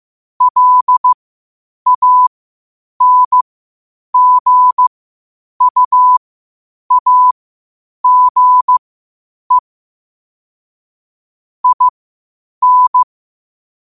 Morse code, heard as a keyed tone, about 15 wpm, spaced as for 8 wpm, 1000 Hz. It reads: LANGUAGE IN